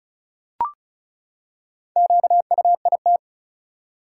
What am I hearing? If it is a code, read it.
Morse code, heard as a keyed tone, about 36 wpm, 700 Hz. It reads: QUIT
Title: Four Letter Words - 35wpm